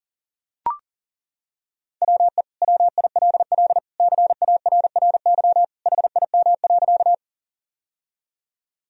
Morse code, heard as a keyed tone, about 37 wpm, 700 Hz. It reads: WE WILL CARRY HIM.
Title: Sentences from Top 1000 Words - Review Part 1 - 40wpm